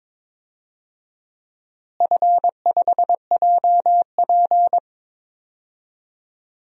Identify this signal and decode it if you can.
Morse code, heard as a keyed tone, about 22 wpm, 700 Hz. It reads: F5JP
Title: DXCC-100 Call Signs - 4 Characters - Repeated 2x - 22wpm